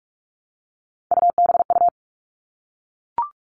Morse code, heard as a keyed tone, about 44 wpm, 700 Hz. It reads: UBU
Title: All Cut Numbers - 3-Digit 45wpm